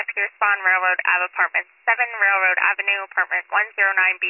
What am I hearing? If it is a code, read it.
Emergency Alert System (SAME) tones, heard as harsh buzzing bursts, with speech. It is heard from 0.5 to 4.1 s.